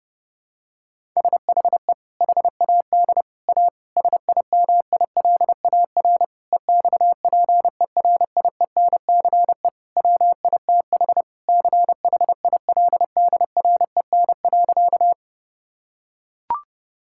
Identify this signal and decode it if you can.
Morse code, heard as a keyed tone, about 30 wpm, 700 Hz. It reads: SHE HAD A SIMILAR EXPERIENCE WITH CHILDREN.
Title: Sentences from Top 1000 Words - Review Part 5 - 30wpm